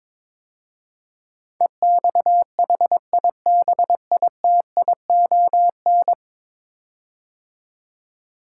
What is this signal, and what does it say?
Morse code, heard as a keyed tone, about 22 wpm, 700 Hz. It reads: EXHIBITION
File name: practice / 22wpm / 10 Letter Words - First 3 Letters Given - 22wpm